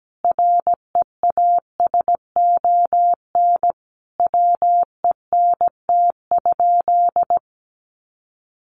Morse code, heard as a keyed tone, about 17 wpm, 700 Hz. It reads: REASON WENT?
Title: Sentences from Top 700 Words 17wpm